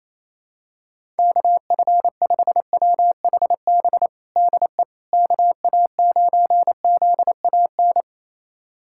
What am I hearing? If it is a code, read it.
Morse code, heard as a keyed tone, about 27 wpm, 700 Hz. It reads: KF5WHB DE KA9ZAN
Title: USA Callsign DE Callsign - 6 Characters - 28wpm